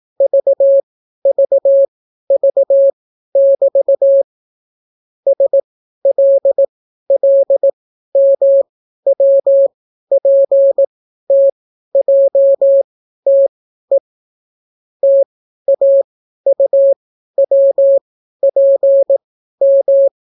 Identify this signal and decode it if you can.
Morse code, heard as a keyed tone, about 18 wpm, spaced as for 12 wpm, 550 Hz. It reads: VVV<BT> SLLMWPTJTE TAUWPM